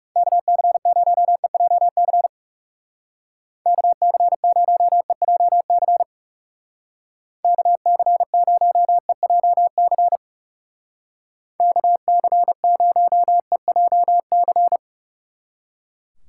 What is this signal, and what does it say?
Morse code, heard as a keyed tone, about 36 wpm, 700 Hz. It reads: KC0EJC KC0EJC KC0EJC KC0EJC
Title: Speed Racing - USA 2x3 Call Signs 30wpm